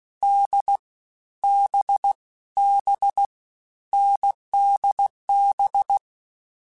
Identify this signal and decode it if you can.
Morse code, heard as a keyed tone, about 16 wpm, 800 Hz. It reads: D B B NDB
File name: Short Runs / Short Runs-mp3s / awj1ndb6gz7 @15